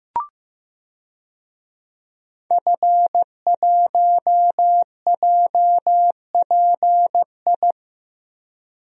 Morse code, heard as a keyed tone, about 15 wpm, 700 Hz. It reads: F1JPI
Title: DXCC-100 Call Signs - 5 Characters - Repeated 2x - 15wpm